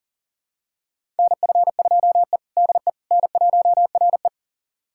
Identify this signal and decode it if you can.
Morse code, heard as a keyed tone, about 40 wpm, 700 Hz. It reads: NF2E DE N1RE
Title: USA Callsign DE Callsign - 4 Characters - 40wpm